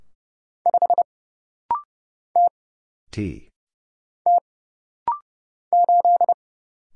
Morse code, heard as a keyed tone, about 30 wpm, 700 Hz. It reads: 5 T T 8
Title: Single Letter-Number - Rapid-Fire 30wpm